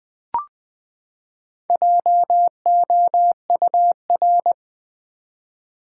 Morse code, heard as a keyed tone, about 20 wpm, 700 Hz. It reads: JOUR
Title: Build A Word - 4 Letters At A Time - 10-Letter Words - 20wpm